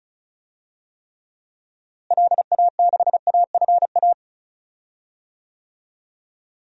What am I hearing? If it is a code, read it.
Morse code, heard as a keyed tone, about 34 wpm, 700 Hz. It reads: LA6AFA
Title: DXCC-100 Call Signs - 6 Characters - Repeated 2x - 35wpm